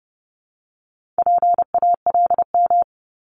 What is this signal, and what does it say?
Morse code, heard as a keyed tone, about 30 wpm, 700 Hz. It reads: PALM
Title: Multi-letter Phonemes Repeated 3x 30wpm